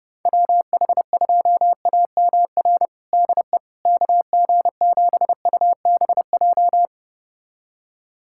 Morse code, heard as a keyed tone, about 30 wpm, 700 Hz. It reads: WH2AMR DE KG7UBJ
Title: USA Callsign DE Callsign - 6 Characters - 30wpm